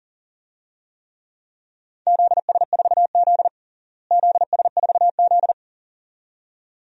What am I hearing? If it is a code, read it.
Morse code, heard as a keyed tone, about 37 wpm, 700 Hz. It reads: ZS4Z ZS4Z
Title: DXCC-100 Call Signs - 4 Characters - Repeated 2x - 40wpm